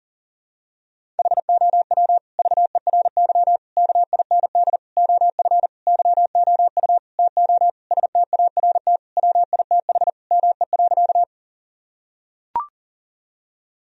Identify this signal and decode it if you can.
Morse code, heard as a keyed tone, about 37 wpm, 700 Hz. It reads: HOW VERY KIND OF YOU TO START WITH ME.
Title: Sentences from Top 1000 Words - Review Part 1 - 40wpm